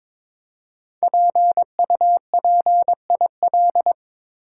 Morse code, heard as a keyed tone, about 22 wpm, 700 Hz. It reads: PUPIL
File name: 5 Letter Words - First 3 Letters Given - 22wpm